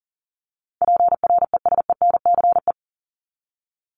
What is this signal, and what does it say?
Morse code, heard as a keyed tone, about 40 wpm, 700 Hz. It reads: PRESENCE
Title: Top 1751-2000 Words 40wpm